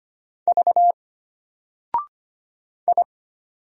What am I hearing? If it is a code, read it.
Morse code, heard as a keyed tone, about 25 wpm, 700 Hz. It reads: V I
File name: Single-Letter-Number-25wpm